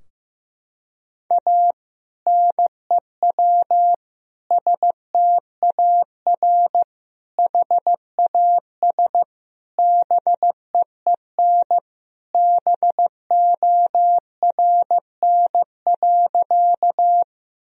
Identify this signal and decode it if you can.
Morse code, heard as a keyed tone, about 15 wpm, 700 Hz. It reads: A NEW STAR HAS BEEN BORN.
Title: Sentences from Top 1000 Words - Review Part 2 - 15wpm